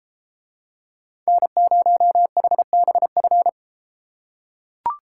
Morse code, heard as a keyed tone, about 33 wpm, 700 Hz. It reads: N0HBF